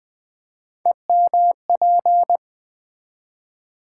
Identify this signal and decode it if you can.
Morse code, heard as a keyed tone, about 20 wpm, 690 Hz. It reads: EMP
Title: Build A Word - 3 Letters At A Time - 6-Letter Words - 20wpm